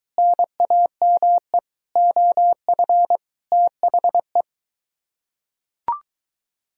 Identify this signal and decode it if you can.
Morse code, heard as a keyed tone, about 22 wpm, 700 Hz. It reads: NAME OF THE